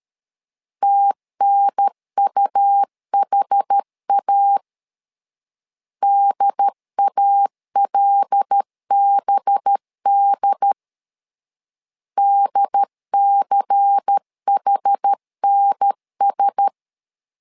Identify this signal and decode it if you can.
Morse code, heard as a keyed tone, about 13 wpm, 800 Hz. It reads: TNUHA DALBD DCHNS
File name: LEZIONI / 5 LBVC / 5 LBVC 12wpm 1955